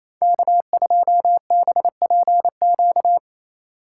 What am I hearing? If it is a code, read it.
Morse code, heard as a keyed tone, about 28 wpm, 700 Hz. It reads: K2BPQ
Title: USA Call Signs - 5 Characters - 2x Character Spacing - 28wpm